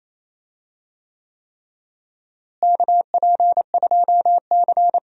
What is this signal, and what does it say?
Morse code, heard as a keyed tone, about 28 wpm, 700 Hz. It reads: KP2C
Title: DXCC-100 Call Signs - 4 Characters - Repeated 3x - 28wpm